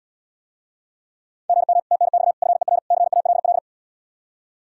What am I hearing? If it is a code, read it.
Morse code, heard as a keyed tone, about 80 wpm, 700 Hz. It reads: GM TNX FER CALL